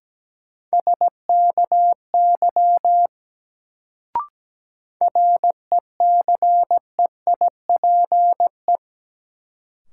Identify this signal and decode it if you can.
Morse code, heard as a keyed tone, about 17 wpm, 700 Hz. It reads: SKY RECEIPE
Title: Sentences from Top 1200 Words - New Word Drill - 17wpm